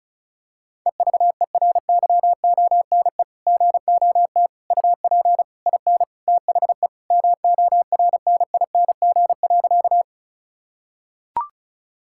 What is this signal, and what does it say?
Morse code, heard as a keyed tone, about 34 wpm, 700 Hz. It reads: EVERYONE GOT UP IN THE MORNING.
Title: Sentences from Top 1000 Words - Review Part 5 - 35wpm